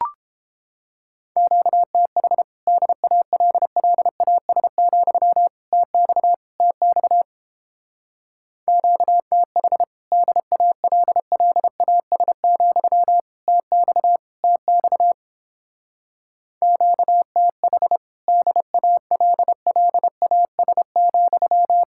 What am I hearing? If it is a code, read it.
Morse code, heard as a keyed tone, about 29 wpm, 700 Hz. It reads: QTH DALLAS, TX TX QTH DALLAS, TX TX QTH DALLAS,